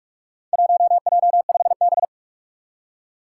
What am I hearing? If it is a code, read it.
Morse code, heard as a keyed tone, about 46 wpm, 700 Hz. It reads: 1J5B
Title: Four Challenging Letter-Number Combinations 45wpm